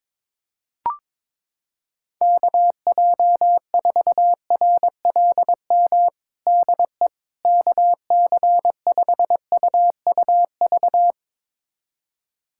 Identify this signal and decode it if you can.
Morse code, heard as a keyed tone, about 22 wpm, 700 Hz. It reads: KJ4RLM DE KC5UUV
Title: USA Callsign DE Callsign - 6 Characters - 22wpm